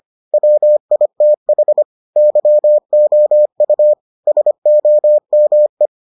Morse code, heard as a keyed tone, about 25 wpm, 600 Hz. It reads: WITH YOU SOME